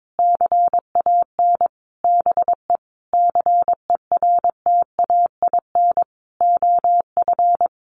Morse code, heard as a keyed tone, about 22 wpm, 700 Hz. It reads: CAN BE CERTAIN OF
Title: Sentences from Top 1100 Words 22wpm